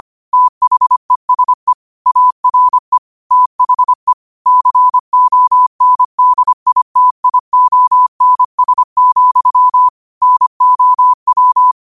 Morse code, heard as a keyed tone, about 25 wpm, 1000 Hz. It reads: THESE ARE THE CONDITIONS, NOW